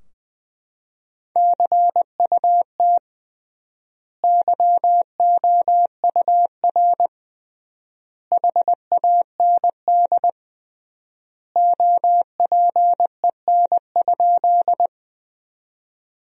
Morse code, heard as a keyed tone, about 21 wpm, 700 Hz. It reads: CUT YOUR HAND OPEN?
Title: 3x Word-Spacing - Sentences from Top 300 Words - 20wpm